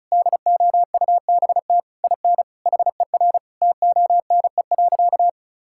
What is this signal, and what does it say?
Morse code, heard as a keyed tone, about 36 wpm, 700 Hz. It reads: DOUBT IN HER TONE.
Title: Sentences from Top 1600 Words 35wpm